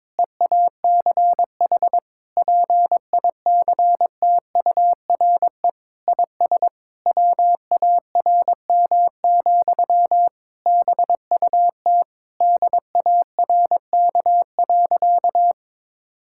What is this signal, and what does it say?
Morse code, heard as a keyed tone, about 22 wpm, 700 Hz. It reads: EACH PICTURE IS WARM, BUT DARK.
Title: Sentences from Top 1600 Words 22wpm